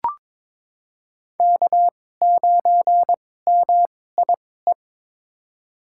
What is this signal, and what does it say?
Morse code, heard as a keyed tone, about 22 wpm, spaced as for 16 wpm, 700 Hz. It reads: K9MIE
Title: USA Call Signs - 5 Characters - 2x Character Spacing - 22wpm